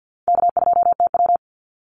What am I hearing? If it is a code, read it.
Morse code, heard as a keyed tone, about 50 wpm, 700 Hz. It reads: K2TW
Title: Easy Call Signs - 50wpm